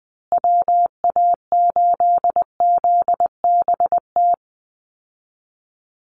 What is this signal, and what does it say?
Morse code, heard as a keyed tone, about 20 wpm, 700 Hz. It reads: WA8ZBT
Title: Call Signs Repeated 3x - 20wpm